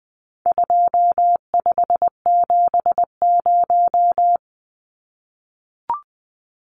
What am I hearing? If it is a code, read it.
Morse code, heard as a keyed tone, about 19 wpm, 700 Hz. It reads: 2570
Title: Numbers - 4-Digit 20wpm